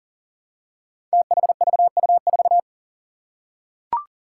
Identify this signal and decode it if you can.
Morse code, heard as a keyed tone, about 40 wpm, 700 Hz. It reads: THVU4
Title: Five Challenging Letter-Number Combinations 40wpm